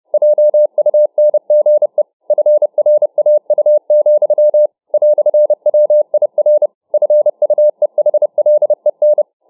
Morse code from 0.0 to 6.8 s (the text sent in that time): JUNGE FRAU, "WIR